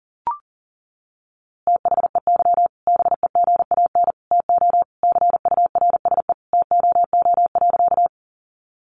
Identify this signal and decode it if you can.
Morse code, heard as a keyed tone, about 37 wpm, 700 Hz. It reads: THEY BEGAN TO CURSE TOO.
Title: A Sound Of Thunder 40wpm